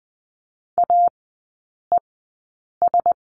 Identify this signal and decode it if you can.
Morse code, heard as a keyed tone, about 20 wpm, spaced as for 8 wpm, 700 Hz. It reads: AES
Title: 035 Lesson 7e - Letter s - Three Characters 20wpm